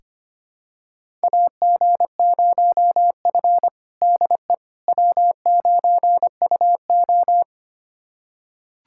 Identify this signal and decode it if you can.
Morse code, heard as a keyed tone, about 25 wpm, 700 Hz. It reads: AG0F DE W9UO
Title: USA Callsign DE Callsign - 4 Characters - 25wpm